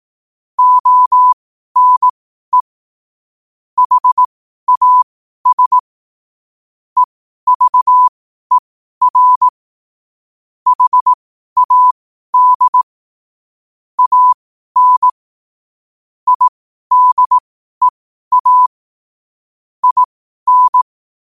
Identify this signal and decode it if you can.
Morse code, heard as a keyed tone, about 18 wpm, spaced as for 12 wpm, 1000 Hz. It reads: ONE HAS EVER HAD AN IDEA IN
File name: qotdfiles / 12wpm